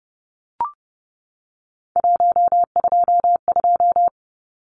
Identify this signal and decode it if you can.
Morse code, heard as a keyed tone, about 30 wpm, 700 Hz. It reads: 122
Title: Numbers - 3-Digit 30wpm